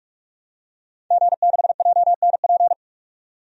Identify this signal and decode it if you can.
Morse code, heard as a keyed tone, about 47 wpm, 700 Hz. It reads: G6JNP
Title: DXCC-100 Call Signs - 5 Characters - 45wpm